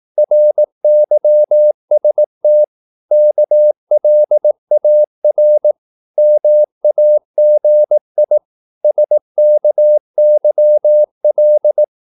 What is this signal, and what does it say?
Morse code, heard as a keyed tone, about 18 wpm, 600 Hz. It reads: RYST KLAR MAGI SKYL
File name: Korte ord DK 18wpm